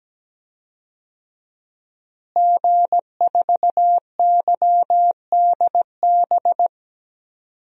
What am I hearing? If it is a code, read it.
Morse code, heard as a keyed tone, about 17 wpm, 700 Hz. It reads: G4YDB
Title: DXCC-100 Call Signs - 5 Characters - Repeated 2x - 17wpm